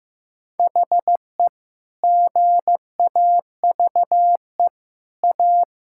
Morse code, heard as a keyed tone, about 15 wpm, 700 Hz. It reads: HE GAVE A
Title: Sentences from Top 1000 Words - Review Part 6 - 15wpm